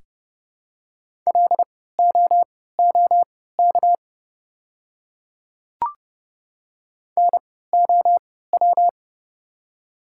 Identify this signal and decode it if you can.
Morse code, heard as a keyed tone, about 30 wpm, spaced as for 17 wpm, 690 Hz. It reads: LOOK NOW
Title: 3x Character Spacing - Top 100 Words - 30wpm